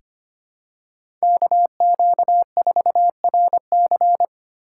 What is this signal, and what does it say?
Morse code, heard as a keyed tone, about 25 wpm, 700 Hz. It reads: KQ4RC
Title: USA Call Signs - 5 Characters - 4x Character Spacing - 25wpm